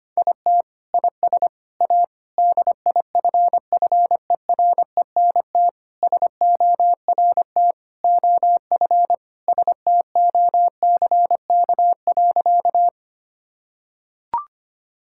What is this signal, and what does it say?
Morse code, heard as a keyed tone, about 25 wpm, 700 Hz. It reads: IT IS A DIFFERENT SORT OF STOCK.
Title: Sentences from Top 1000 Words - Review Part 5 - 25wpm